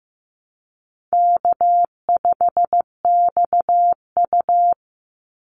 Morse code, heard as a keyed tone, about 14 wpm, 700 Hz. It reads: K5XU
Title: Easy Call Signs Repeated 2x - 15wpm